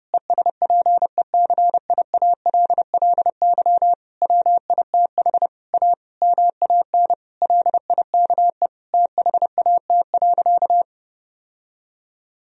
Morse code, heard as a keyed tone, about 30 wpm, 700 Hz. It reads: ESPECIALLY WITH A MAN LIKE THAT.
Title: Sentences from Top 1000 Words - Review Part 4 - 30wpm